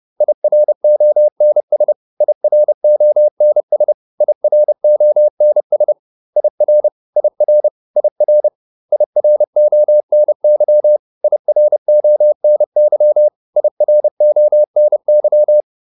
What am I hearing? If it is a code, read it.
Morse code, heard as a keyed tone, about 30 wpm, 600 Hz. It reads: IRONS IRONS IRONS IR IR IR IRONY IRONY IRONY